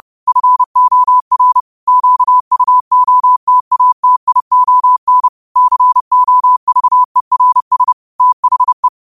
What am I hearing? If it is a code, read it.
Morse code, heard as a keyed tone, about 30 wpm, 1000 Hz. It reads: FOR QUOTATION COVERS THE
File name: qotdfiles / 30wpm